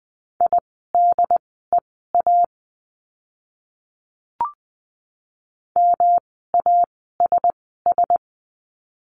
Morse code, heard as a keyed tone, about 20 wpm, spaced as for 15 wpm, 700 Hz. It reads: IDEA MASS
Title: 108 Lesson 18f - Letter m - Words 20wpm